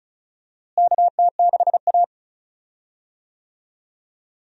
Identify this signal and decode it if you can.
Morse code, heard as a keyed tone, about 36 wpm, 700 Hz. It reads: KT6A
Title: USA Call Signs - 4 Characters - Repeated 2x - 35wpm